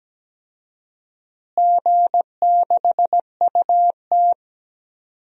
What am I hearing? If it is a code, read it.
Morse code, heard as a keyed tone, about 17 wpm, 700 Hz. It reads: G6UT
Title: DXCC-100 Call Signs - 4 Characters - Repeated 3x - 17wpm